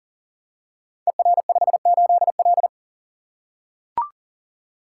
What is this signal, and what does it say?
Morse code, heard as a keyed tone, about 41 wpm, 700 Hz. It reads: ER58L
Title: Code Groups - 5 Letter:Numbers - 40wpm